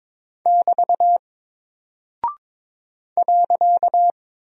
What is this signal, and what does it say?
Morse code, heard as a keyed tone, about 22 wpm, 700 Hz. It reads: <BT> .
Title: Single Character (Letter:Number:Punctuation:Prosign) Alt 22wpm